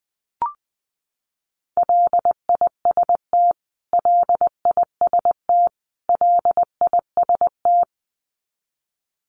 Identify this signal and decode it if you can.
Morse code, heard as a keyed tone, about 20 wpm, 700 Hz. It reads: LIST LIST LIST
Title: Top 500 Words Repeated 20wpm